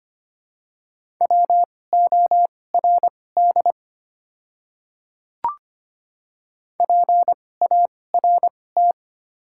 Morse code, heard as a keyed tone, about 25 wpm, spaced as for 18 wpm, 700 Hz. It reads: WORD PART
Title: 2x Character Spacing - Top 200 Words - 25wpm